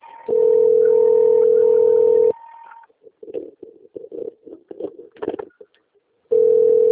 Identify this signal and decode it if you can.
Ringback tone (North America), 440 + 480 Hz.